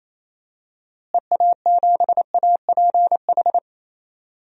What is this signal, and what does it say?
Morse code, heard as a keyed tone, about 28 wpm, 700 Hz. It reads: EA7APH
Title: DXCC-100 Call Signs - 6 Characters - Repeated 3x - 28wpm